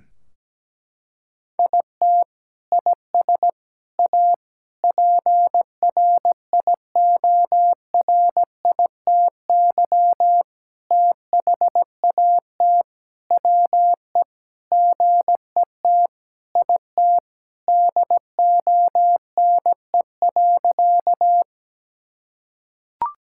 Morse code, heard as a keyed tone, about 17 wpm, 700 Hz. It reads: IT IS A PRIORITY THAT WE GET IT DONE.
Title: Sentences from Top 1600 Words 17wpm